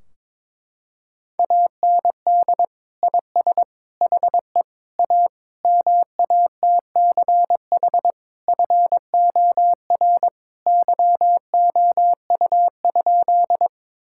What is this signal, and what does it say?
Morse code, heard as a keyed tone, about 21 wpm, 700 Hz. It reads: AND IS HE A MATCH FOR YOU?
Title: Sentences from Top 1600 Words 22wpm